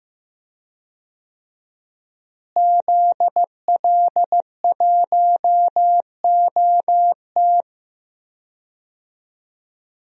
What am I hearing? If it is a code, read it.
Morse code, heard as a keyed tone, about 15 wpm, 700 Hz. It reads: ZL1OT
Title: DXCC-100 Call Signs - 5 Characters - Repeated 2x - 15wpm